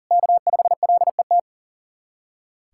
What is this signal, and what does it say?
Morse code, heard as a keyed tone, about 37 wpm, 700 Hz. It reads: K5LET
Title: USA Callsign DE Callsign - 5 Characters - 40wpm